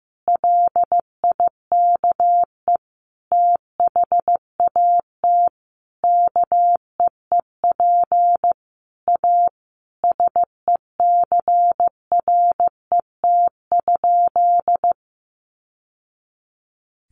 Morse code, heard as a keyed tone, about 15 wpm, 700 Hz. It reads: LIKE THAT KEEP A SECRET?
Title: Sentences from Top 1200 Words 15wpm